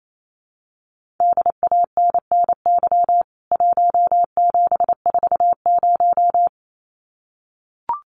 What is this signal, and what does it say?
Morse code, heard as a keyed tone, about 28 wpm, 700 Hz. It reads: DANNY 1740
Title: CWT Exchange 28wpm